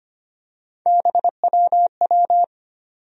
Morse code, heard as a keyed tone, about 25 wpm, 700 Hz. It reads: BWW
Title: USA Call Sign Postfixes - 3 Letters 25wpm